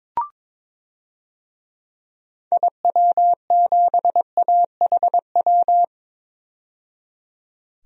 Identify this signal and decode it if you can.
Morse code, heard as a keyed tone, about 22 wpm, 700 Hz. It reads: IW7AHW
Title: DXCC-100 Call Signs - 6 Characters - 22wpm